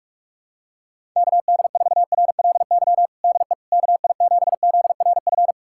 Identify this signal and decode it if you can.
Morse code, heard as a keyed tone, about 46 wpm, 690 Hz. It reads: KD4RLY DE KI7ZRF
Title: USA Callsign DE Callsign 45wpm